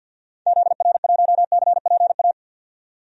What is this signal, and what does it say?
Morse code, heard as a keyed tone, about 52 wpm, 700 Hz. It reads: ZR1CPA
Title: DXCC-100 Call Signs - 6 Characters - 50wpm